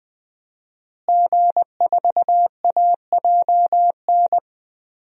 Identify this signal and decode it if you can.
Morse code, heard as a keyed tone, about 20 wpm, 700 Hz. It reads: G4AJN
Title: DXCC-100 Call Signs - 5 Characters - Repeated 2x - 20wpm